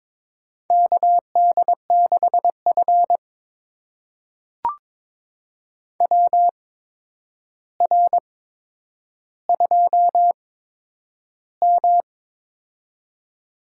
Morse code, heard as a keyed tone, about 22 wpm, spaced as for 9 wpm, 700 Hz. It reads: KD6F WR2M
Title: USA Call Signs - 4 Characters - 8x Character Spacing - 22wpm